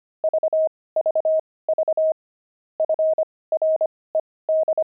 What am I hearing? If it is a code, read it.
Morse code, heard as a keyed tone, about 25 wpm, spaced as for 18 wpm, 630 Hz. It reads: VVV FRED
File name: pangram-fs-18-cs-25-toys